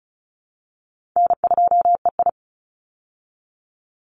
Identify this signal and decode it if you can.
Morse code, heard as a keyed tone, about 34 wpm, 700 Hz. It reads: N2EI
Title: Call Signs Repeated 2x - 35wpm